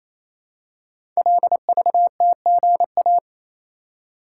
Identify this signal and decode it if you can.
Morse code, heard as a keyed tone, about 29 wpm, 700 Hz. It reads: LVTGA
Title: Code Groups - 5 Letters - 28wpm